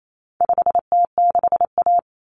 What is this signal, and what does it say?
Morse code, heard as a keyed tone, about 28 wpm, 700 Hz. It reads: 5T6A
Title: Common Cut Numbers - 4-Digit 28wpm